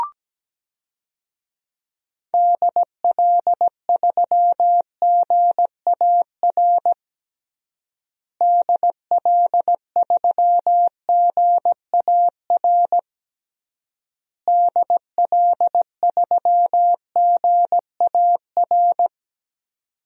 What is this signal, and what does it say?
Morse code, heard as a keyed tone, about 17 wpm, 700 Hz. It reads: DL3GAR DL3GAR DL3GAR
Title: DXCC-100 Call Signs - 6 Characters - Repeated 3x - 17wpm